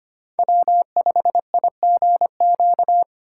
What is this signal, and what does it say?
Morse code, heard as a keyed tone, about 25 wpm, 700 Hz. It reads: W5IGQ
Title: USA Call Signs - 5 Characters - 2x Character Spacing - 25wpm